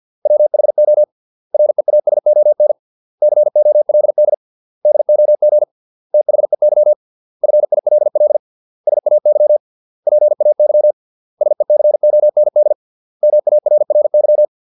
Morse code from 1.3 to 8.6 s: REASON COLD DOG THEY FILL